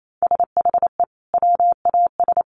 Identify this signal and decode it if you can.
Morse code, heard as a keyed tone, about 28 wpm, 700 Hz. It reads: SHE WAS
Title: Sentences from Top 1300 Words 28wpm